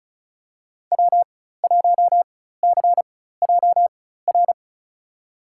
Morse code, heard as a keyed tone, about 34 wpm, spaced as for 16 wpm, 700 Hz. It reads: W1CJR
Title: USA Call Signs - 5 Characters - 4x Character Spacing - 35wpm